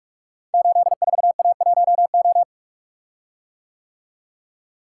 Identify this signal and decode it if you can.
Morse code, heard as a keyed tone, about 44 wpm, 700 Hz. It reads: 84A1O
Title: Code Groups - 5 Letter:Numbers - 45wpm